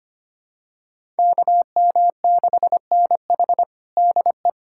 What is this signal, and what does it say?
Morse code, heard as a keyed tone, about 25 wpm, 700 Hz. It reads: KM6NH DE
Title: USA Callsign DE Callsign - 5 Characters - 25wpm